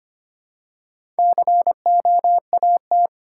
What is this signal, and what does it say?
Morse code, heard as a keyed tone, about 25 wpm, 690 Hz. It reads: COAT
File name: practice / 25wpm / Easy Words with Hints - 25wpm